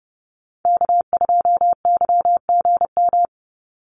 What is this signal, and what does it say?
Morse code, heard as a keyed tone, about 30 wpm, 700 Hz. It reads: K2YGM
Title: Call Signs Repeated 3x - 30wpm